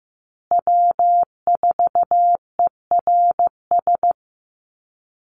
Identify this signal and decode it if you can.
Morse code, heard as a keyed tone, about 15 wpm, 700 Hz. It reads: W4ERS
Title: USA Call Signs Repeated 2x 15wpm